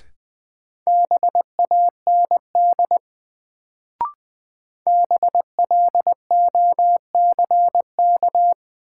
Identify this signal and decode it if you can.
Morse code, heard as a keyed tone, about 20 wpm, 700 Hz. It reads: BAND BLOCK
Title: Sentences from Top 1600 Words - New Word Drill - 20wpm